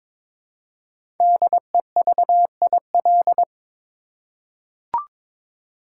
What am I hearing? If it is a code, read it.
Morse code, heard as a keyed tone, about 22 wpm, 700 Hz. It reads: DEVIL
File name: 5 Letter Words - First 3 Letters Given - 22wpm